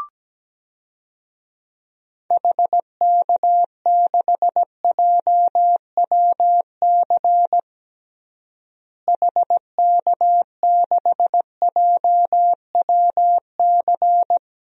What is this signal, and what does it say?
Morse code, heard as a keyed tone, about 17 wpm, 700 Hz. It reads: HK6JWC HK6JWC
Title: DXCC-100 Call Signs - 6 Characters - Repeated 3x - 17wpm